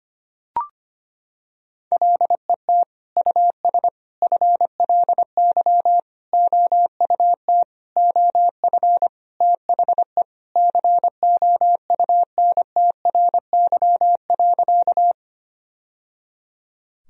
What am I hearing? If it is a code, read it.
Morse code, heard as a keyed tone, about 25 wpm, 700 Hz. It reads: LET US FLY OUT OF THE COUNTRY.
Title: Sentences from Top 1000 Words - Review Part 2 - 25wpm